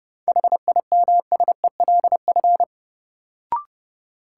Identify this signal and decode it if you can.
Morse code, heard as a keyed tone, about 31 wpm, 700 Hz. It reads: HIMSELF
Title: Sentences from Top 1500 Words - New Word Drill - 30wpm